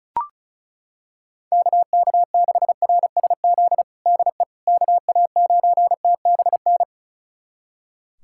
Morse code, heard as a keyed tone, about 34 wpm, 700 Hz. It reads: KK6RSZ DE KA9TBN
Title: USA Callsign DE Callsign 35wpm